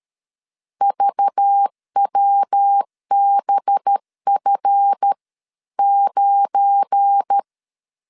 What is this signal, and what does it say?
Morse code, heard as a keyed tone, about 13 wpm, spaced as for 11 wpm, 800 Hz. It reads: VWBF9
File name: LEZIONI / 10 SIGNS / 10C DOT 12wpm 2451